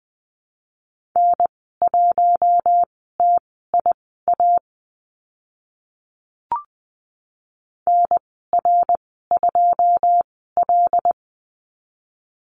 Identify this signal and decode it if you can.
Morse code, heard as a keyed tone, about 20 wpm, spaced as for 14 wpm, 700 Hz. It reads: N1TIA NR2L
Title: 082 Lesson 14g - Number 2 - Call Signs 20wpm